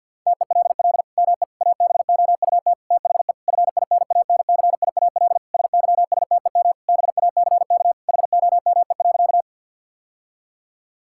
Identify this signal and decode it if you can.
Morse code, heard as a keyed tone, about 50 wpm, 700 Hz. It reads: TELL ME ABOUT THE FINANCIAL SYSTEM BACK HOME.
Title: Sentences from Top 1000 Words - Review Part 5 - 50wpm